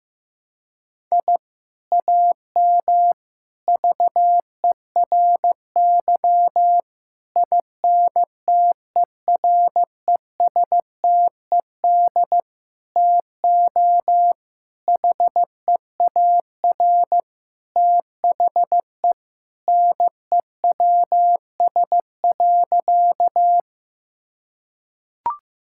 Morse code, heard as a keyed tone, about 15 wpm, 700 Hz. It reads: I AM VERY INTERESTED TO HEAR THE NEWS.
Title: Sentences from Top 1000 Words - Review Part 6 - 15wpm